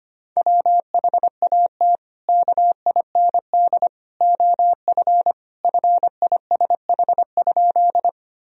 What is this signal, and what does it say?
Morse code, heard as a keyed tone, about 25 wpm, 700 Hz. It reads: WHAT KIND OF FISH?
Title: Sentences from Top 1000 Words - Review Part 1 - 25wpm